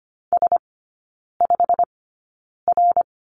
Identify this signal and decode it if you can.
Morse code, heard as a keyed tone, about 25 wpm, spaced as for 9 wpm, 690 Hz. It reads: S5R
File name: Lesson 15e - Number 5 - Three Characters 25wpm